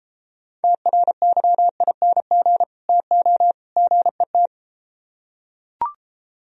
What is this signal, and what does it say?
Morse code, heard as a keyed tone, about 33 wpm, 700 Hz. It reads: TRYING TO GET